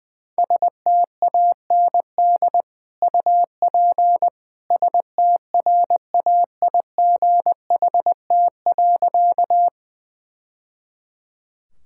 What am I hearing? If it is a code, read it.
Morse code, heard as a keyed tone, about 21 wpm, 700 Hz. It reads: STAND UP STRAIGHT.
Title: Sentences from Top 1000 Words - Review Part 6 - 20wpm